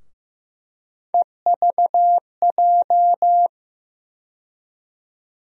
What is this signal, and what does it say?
Morse code, heard as a keyed tone, about 15 wpm, 700 Hz. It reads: EVJ
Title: USA Call Sign Postfixes - 3 Letters 15wpm